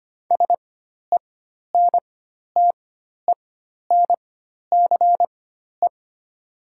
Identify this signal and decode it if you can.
Morse code, heard as a keyed tone, about 25 wpm, spaced as for 12 wpm, 700 Hz. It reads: SENTENCE
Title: 4x Character Spacing - Top 200 Words - 25wpm